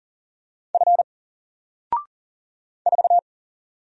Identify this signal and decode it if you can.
Morse code, heard as a keyed tone, about 40 wpm, 700 Hz. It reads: F 4
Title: Single Character (Letter:Number:Punctuation:Prosign) 40wpm